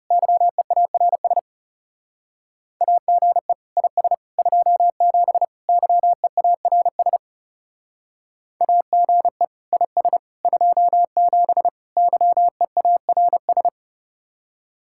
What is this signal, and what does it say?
Morse code, heard as a keyed tone, about 37 wpm, 700 Hz. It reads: YEARS AGE IS 27 YEARS AGE IS 27 YEARS
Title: Speed-Racing Common QSO Phrases Encore - 30wpm